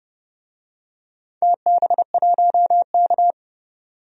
Morse code, heard as a keyed tone, about 30 wpm, 700 Hz. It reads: TB1K
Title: DXCC-100 Call Signs - 4 Characters - 30wpm